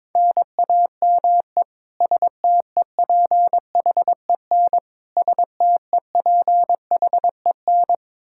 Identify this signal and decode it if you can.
Morse code, heard as a keyed tone, about 22 wpm, 700 Hz. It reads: NAME STEPHEN STEPHEN
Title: Practice QSO - Exchange 1 - Beginning - 22wpm